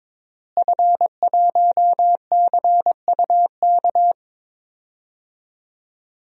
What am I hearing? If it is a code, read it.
Morse code, heard as a keyed tone, about 22 wpm, 700 Hz. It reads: F1CUK
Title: DXCC-100 Call Signs - 5 Characters - Repeated 3x - 22wpm